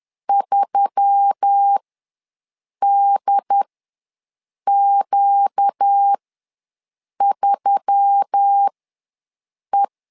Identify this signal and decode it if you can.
Morse code, heard as a keyed tone, about 10 wpm, spaced as for 6 wpm, 800 Hz. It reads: 3DQ3E
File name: LEZIONI / 4 Q1S3 ws 10wpm 2437